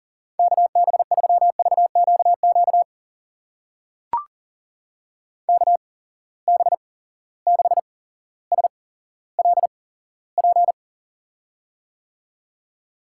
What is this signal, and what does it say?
Morse code, heard as a keyed tone, about 38 wpm, spaced as for 17 wpm, 700 Hz. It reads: KB3VQQ KB6SLP
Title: USA Call Signs - 6 Characters - 8x Character Spacing - 40wpm